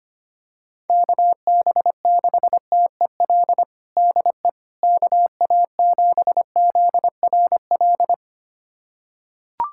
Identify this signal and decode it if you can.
Morse code, heard as a keyed tone, about 25 wpm, 700 Hz. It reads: KB6TEL DE KA7ZRL
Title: USA Callsign DE Callsign 25wpm